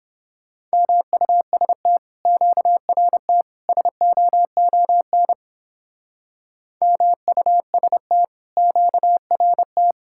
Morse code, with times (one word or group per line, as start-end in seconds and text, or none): 0.7-2.0 MUST
2.3-3.4 QRT
3.7-5.3 SOON
6.8-8.2 MUST
8.6-9.9 QRT